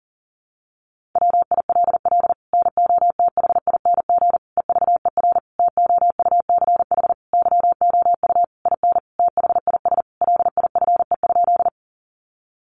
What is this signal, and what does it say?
Morse code, heard as a keyed tone, about 40 wpm, 700 Hz. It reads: WILL NOTHING EVER TOUCH YOU IN THIS LIFE?
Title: Sentences from Top 1200 Words 40wpm